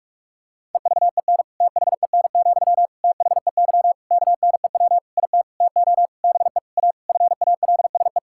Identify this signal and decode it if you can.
Morse code, heard as a keyed tone, about 46 wpm, 700 Hz. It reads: EVEN THEN, THEY KNEW IT TO BE A FALSE
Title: Sentences from Top 1600 Words 45wpm